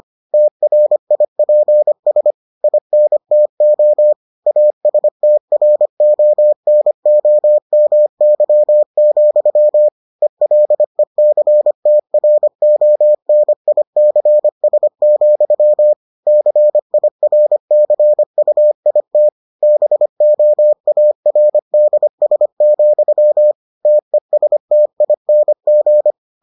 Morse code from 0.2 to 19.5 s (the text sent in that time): TRIPS INTO ASTRONOMY, ELECTRONICS, CIRCUIT